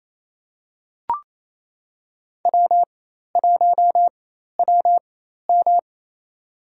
Morse code, heard as a keyed tone, about 28 wpm, spaced as for 13 wpm, 700 Hz. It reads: W1WM
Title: USA Call Signs - 4 Characters - 4x Character Spacing - 28wpm